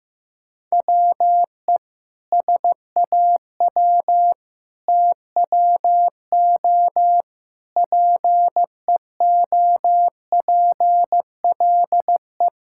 Morse code, heard as a keyed tone, about 15 wpm, 700 Hz. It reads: WE SAW TWO PEOPLE
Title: Sentences from Top 1000 Words - Review Part 6 - 15wpm